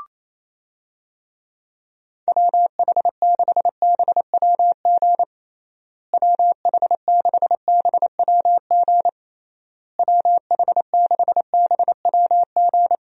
Morse code, heard as a keyed tone, about 27 wpm, 700 Hz. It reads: WH6BWG WH6BWG WH6BWG
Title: USA Call Signs - 6 Characters - Repeated 3x - 28wpm